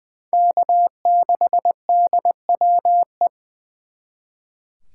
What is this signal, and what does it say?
Morse code, heard as a keyed tone, about 23 wpm, 700 Hz. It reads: K6DWE
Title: Speed Racing - USA 1x3 Call Signs 20wpm